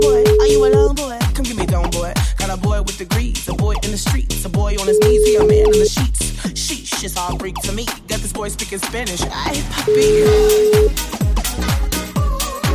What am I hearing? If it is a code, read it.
Ringback tone (Europe, 425 Hz), with speech.